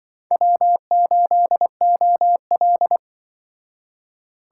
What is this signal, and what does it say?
Morse code, heard as a keyed tone, about 25 wpm, 700 Hz. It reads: W8OL